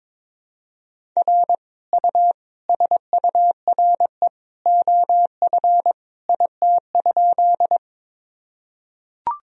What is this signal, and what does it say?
Morse code, heard as a keyed tone, about 22 wpm, 700 Hz. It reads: R U SURE OF IT?
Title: Sentences from Top 300 Words and Abbreviations 22wpm